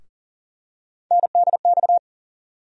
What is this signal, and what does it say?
Morse code, heard as a keyed tone, about 40 wpm, 700 Hz. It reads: NDX
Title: USA Call Sign Postfixes - 3 Letters 40wpm